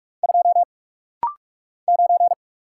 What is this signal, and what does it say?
Morse code, heard as a keyed tone, about 44 wpm, 700 Hz. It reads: 2 9
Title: Single Letter-Number - Rapid-Fire 45wpm